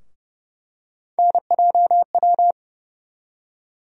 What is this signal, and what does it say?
Morse code, heard as a keyed tone, about 30 wpm, 700 Hz. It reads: NJW
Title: USA Call Sign Postfixes - 3 Letters 30wpm